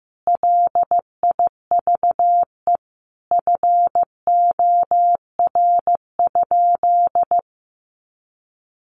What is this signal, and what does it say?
Morse code, heard as a keyed tone, about 14 wpm, 700 Hz. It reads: LIVE FOR?
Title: Sentences from Top 200 Words 15wpm